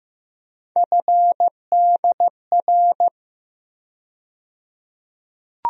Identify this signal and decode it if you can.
Morse code, heard as a keyed tone, about 15 wpm, 700 Hz. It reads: FDR
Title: USA Call Sign Postfixes - 3 Letters 15wpm